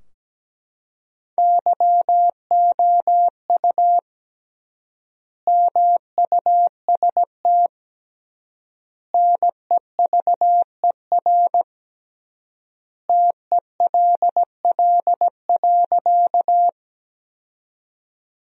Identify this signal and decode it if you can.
Morse code, heard as a keyed tone, about 17 wpm, 700 Hz. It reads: YOU MUST NEVER TELL.
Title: 3x Word-Spacing - Sentences from Top 300 Words - 17wpm